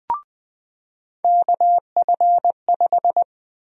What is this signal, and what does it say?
Morse code, heard as a keyed tone, about 21 wpm, 700 Hz. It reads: KF5
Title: USA Call Sign Prefixes - 20wpm